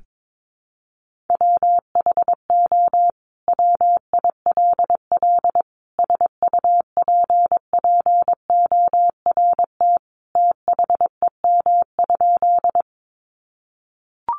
Morse code, heard as a keyed tone, about 22 wpm, 700 Hz. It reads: WHO WILL SUPPORT THEM?
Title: Sentences from Top 800 Words 22wpm